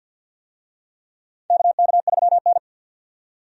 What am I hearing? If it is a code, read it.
Morse code, heard as a keyed tone, about 51 wpm, 700 Hz. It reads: KK3N
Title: USA Call Signs - 4 Characters - 50wpm